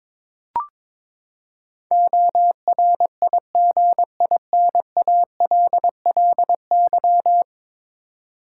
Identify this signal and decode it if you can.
Morse code, heard as a keyed tone, about 22 wpm, 700 Hz. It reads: ORIGINALLY
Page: morse-code-ninja-practice-22wpm